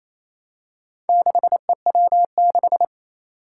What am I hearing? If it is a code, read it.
Morse code, heard as a keyed tone, about 28 wpm, 700 Hz. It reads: 6EW6
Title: Four Challenging Letter-Number Combinations 28wpm